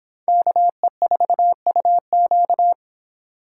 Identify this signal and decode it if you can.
Morse code, heard as a keyed tone, about 26 wpm, 700 Hz. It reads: KE4UQ